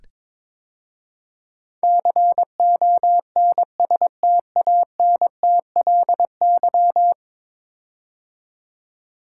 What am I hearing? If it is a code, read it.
Morse code, heard as a keyed tone, about 21 wpm, 700 Hz. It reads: CONSTANTLY
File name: practice / 22wpm / 10 Letter Words - First 3 Letters Given - 22wpm